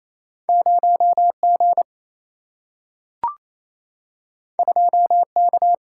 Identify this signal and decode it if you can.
Morse code, heard as a keyed tone, about 28 wpm, 700 Hz. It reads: 0G 2K
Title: Single-Letter Single-Number 28wpm